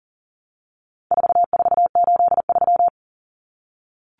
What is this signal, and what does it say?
Morse code, heard as a keyed tone, about 40 wpm, 700 Hz. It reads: 4483
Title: Common Cut Numbers - 4-Digit 40wpm